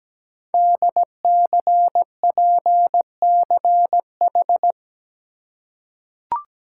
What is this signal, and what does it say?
Morse code, heard as a keyed tone, about 17 wpm, 700 Hz. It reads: DCPCH
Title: Code Groups - 5 Letters - 17wpm